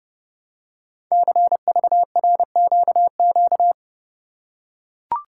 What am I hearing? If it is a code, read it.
Morse code, heard as a keyed tone, about 30 wpm, 700 Hz. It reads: CVRQQ
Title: Code Groups - 5 Letters - 30wpm